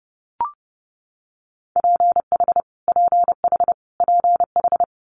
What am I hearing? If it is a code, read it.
Morse code, heard as a keyed tone, about 30 wpm, 700 Hz. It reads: PH PH PH
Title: Multi-letter Phonemes Repeated 3x 30wpm